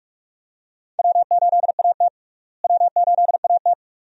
Morse code, heard as a keyed tone, about 41 wpm, 700 Hz. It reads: W8AT W8AT
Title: USA Call Signs - 4 Characters - Repeated 2x - 45wpm